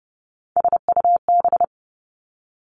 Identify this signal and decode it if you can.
Morse code, heard as a keyed tone, about 30 wpm, 700 Hz. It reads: SUB
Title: Three Challenging Letter-Number Combinations 30wpm